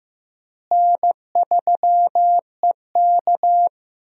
Morse code, heard as a keyed tone, about 15 wpm, 700 Hz. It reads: N3EK
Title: USA Call Signs - 4 Characters - Repeated 2x - 15wpm